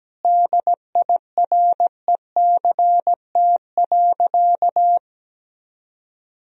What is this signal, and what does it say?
Morse code, heard as a keyed tone, about 17 wpm, 700 Hz. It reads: DIRECT.
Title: Sentences from Top 1000 Words - Review Part 2 - 17wpm